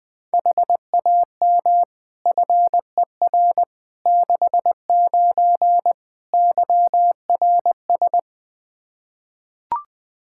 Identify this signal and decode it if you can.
Morse code, heard as a keyed tone, about 20 wpm, 700 Hz. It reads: HAM FER 69 YRS
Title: K1USN SST QSO Practice 20wpm